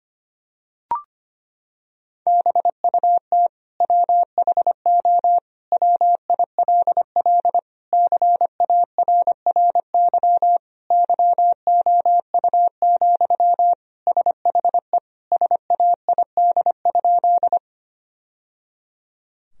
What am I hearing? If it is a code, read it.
Morse code, heard as a keyed tone, about 25 wpm, 700 Hz. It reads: BUT WHO WILL CARRY YOU, SHE SAID?
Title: Sentences from Top 1000 Words - Review Part 1 - 25wpm